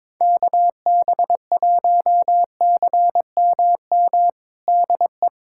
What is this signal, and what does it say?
Morse code, heard as a keyed tone, about 22 wpm, 700 Hz. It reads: KB1CMM DE
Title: USA Callsign DE Callsign - 6 Characters - 22wpm